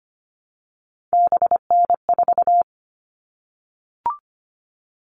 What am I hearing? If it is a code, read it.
Morse code, heard as a keyed tone, about 25 wpm, 700 Hz. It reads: BN4
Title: All Cut Numbers - 3-Digit 25wpm